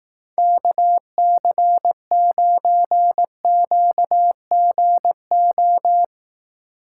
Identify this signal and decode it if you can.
Morse code, heard as a keyed tone, about 20 wpm, 700 Hz. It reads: KC9QGO